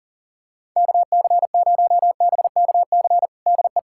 Morse code, heard as a keyed tone, about 40 wpm, 700 Hz. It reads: KC0BKC DE
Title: USA Callsign DE Callsign - 6 Characters - 40wpm